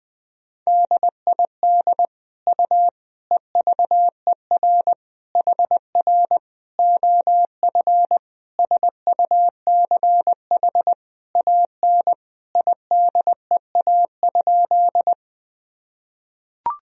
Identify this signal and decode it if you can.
Morse code, heard as a keyed tone, about 21 wpm, 700 Hz. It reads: DID U EVER HR OF SUCH AN IDEA?
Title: Sentences from Top 400 Words and Abbreviations 20wpm